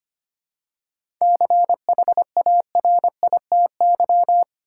Morse code, heard as a keyed tone, about 25 wpm, 700 Hz. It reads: CHARITY
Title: Seven Letter Words - 25wpm